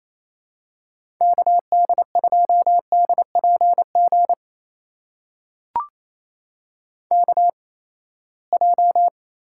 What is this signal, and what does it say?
Morse code, heard as a keyed tone, about 28 wpm, spaced as for 19 wpm, 700 Hz. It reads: KD2DPG KJ
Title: USA Call Signs - 6 Characters - 8x Character Spacing - 28wpm